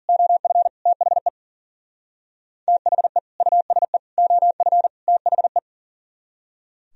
Morse code, heard as a keyed tone, about 50 wpm, 700 Hz. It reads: OF THE THE USE OF THE